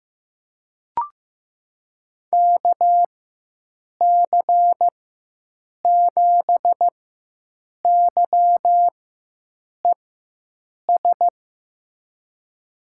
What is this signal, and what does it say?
Morse code, heard as a keyed tone, about 15 wpm, spaced as for 7 wpm, 700 Hz. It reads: KC7YES
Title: USA Call Signs - 4x Character Spacing - 15wpm